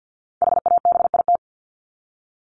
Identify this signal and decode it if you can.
Morse code, heard as a keyed tone, about 50 wpm, 700 Hz. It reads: HABIT
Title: Top 1751-2000 Words 50wpm